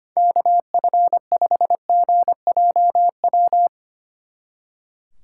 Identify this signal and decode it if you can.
Morse code, heard as a keyed tone, about 31 wpm, 700 Hz. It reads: KF5GJW
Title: Speed Racing - USA 2x3 Call Signs 25wpm